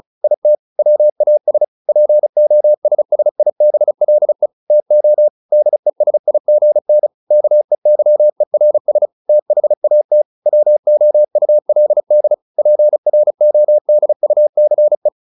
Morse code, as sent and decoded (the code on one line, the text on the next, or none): .. - / .-- .- ... / .--. --- ... ... .. -... .-.. / - --- / -.. ... .. --. -. / -.- . -.-- . .-. ... / - .... .- - / .-- --- ..- .-.. -.. / .--. .-. --- -.. ..- -.-. .
IT WAS POSSIBLE TO DESIGN KEYERS THAT WOULD PRODUCE